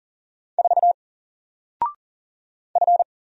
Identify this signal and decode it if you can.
Morse code, heard as a keyed tone, about 40 wpm, 700 Hz. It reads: 4 F
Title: Single Letter-Number Alt 40wpm